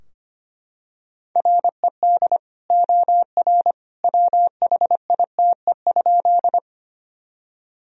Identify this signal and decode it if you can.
Morse code, heard as a keyed tone, about 25 wpm, 700 Hz. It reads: RED OR WHITE?
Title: Sentences from Top 1000 Words - Review Part 2 - 25wpm